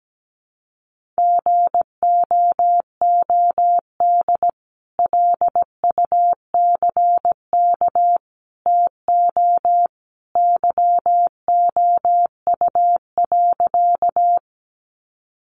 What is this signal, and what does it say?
Morse code, heard as a keyed tone, about 17 wpm, 700 Hz. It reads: GOOD LUCK TO YOU.
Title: Sentences from Top 1500 Words 17wpm